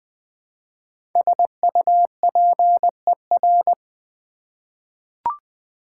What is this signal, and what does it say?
Morse code, heard as a keyed tone, about 21 wpm, 700 Hz. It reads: SUPER
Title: Prefixes and Suffixes 20wpm